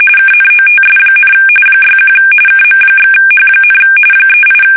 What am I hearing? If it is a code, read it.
RTTY, about 45.45 baud. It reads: KNOTS, 61 KM/H, 38
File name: fsk850 / battleships